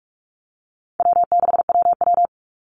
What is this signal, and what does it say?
Morse code, heard as a keyed tone, about 44 wpm, 700 Hz. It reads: W6WW
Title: NA SOTA Chaser Callsigns 45wpm